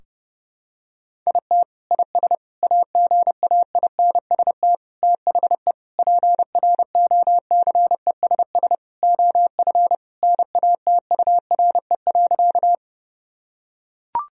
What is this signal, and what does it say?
Morse code, heard as a keyed tone, about 32 wpm, 700 Hz. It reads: IT IS AGAINST THE PROCESS OF NATURE.
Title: Sentences from Top 1000 Words - Review Part 3 - 30wpm